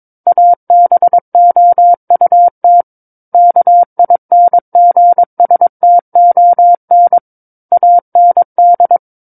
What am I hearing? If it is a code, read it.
Morse code, heard as a keyed tone, about 22 wpm, 700 Hz. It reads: ABOUT KINGSTON AND
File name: Book One / 13-Chapter 13 - 22 WPM